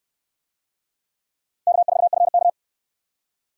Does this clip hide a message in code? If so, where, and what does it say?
Yes: Morse code from 1.7 to 2.5 s, heard as a keyed tone, about 59 wpm, 700 Hz. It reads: K4FK